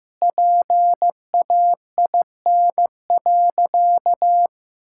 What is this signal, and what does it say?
Morse code, heard as a keyed tone, about 15 wpm, 700 Hz. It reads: PAIN.
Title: Sentences from Top 1600 Words 15wpm